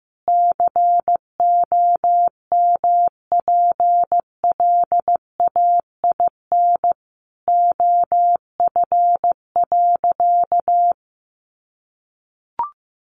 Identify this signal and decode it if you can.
Morse code, heard as a keyed tone, about 15 wpm, 700 Hz. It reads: COMPLAIN OF.
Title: Sentences from Top 1300 Words 15wpm